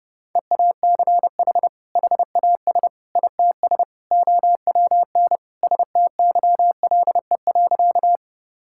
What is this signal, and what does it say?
Morse code, heard as a keyed tone, about 30 wpm, 700 Hz. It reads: EACH HAS ITS OWN STYLE.
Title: Sentences from Top 1000 Words - Review Part 5 - 30wpm